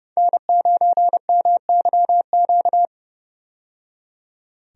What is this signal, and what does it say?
Morse code, heard as a keyed tone, about 32 wpm, 700 Hz. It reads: N9MYQ
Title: USA Call Signs - 5 Characters - 30wpm